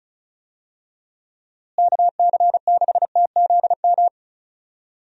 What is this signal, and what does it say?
Morse code, heard as a keyed tone, about 36 wpm, 700 Hz. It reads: KC6TZM
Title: USA Call Signs - 6 Characters - Repeated 2x - 35wpm